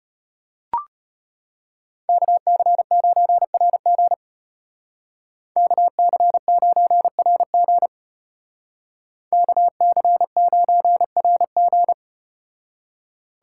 Morse code, heard as a keyed tone, about 34 wpm, 700 Hz. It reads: KC9RG KC9RG KC9RG